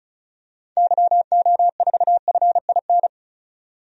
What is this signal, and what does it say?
Morse code, heard as a keyed tone, about 34 wpm, 700 Hz. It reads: YO4FIN
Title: DXCC-100 Call Signs - 6 Characters - 35wpm